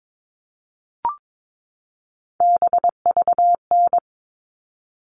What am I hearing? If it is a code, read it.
Morse code, heard as a keyed tone, about 22 wpm, 700 Hz. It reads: BVN
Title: All Cut Numbers - 3-Digit 22wpm